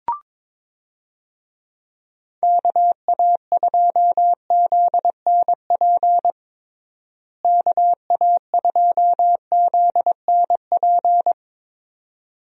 Morse code, heard as a keyed tone, about 23 wpm, 700 Hz. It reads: KA2ZNP KA2ZNP
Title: USA Call Signs - 6 Characters - Repeated 2x - 22wpm